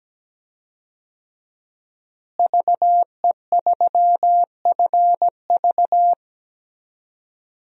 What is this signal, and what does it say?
Morse code, heard as a keyed tone, about 17 wpm, 700 Hz. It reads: VE3FV
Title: DXCC-100 Call Signs - 5 Characters - Repeated 3x - 17wpm